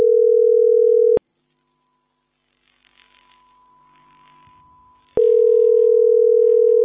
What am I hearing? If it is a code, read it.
Ringback tone (North America), 440 + 480 Hz.